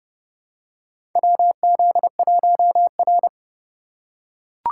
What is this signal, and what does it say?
Morse code, heard as a keyed tone, about 30 wpm, 700 Hz. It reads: WZ1R
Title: USA Call Signs - 4 Characters - 8x Character Spacing - 30wpm